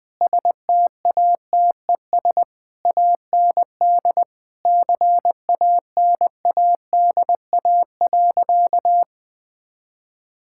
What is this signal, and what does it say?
Morse code, heard as a keyed tone, about 20 wpm, 700 Hz. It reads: STATES AND CANADA.
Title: Sentences with 5-Word Phrases - 20wpm